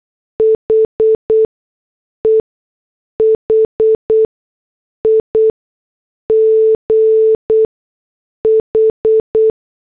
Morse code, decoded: HE HIGH